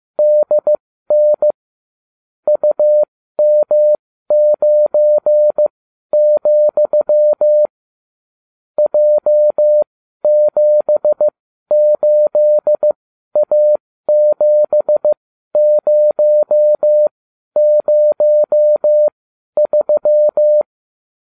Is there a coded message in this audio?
Yes: Morse code from 0.2 to 20.6 s, heard as a keyed tone, about 15 wpm, spaced as for 12 wpm, 600 Hz. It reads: DN UM9, J78A7003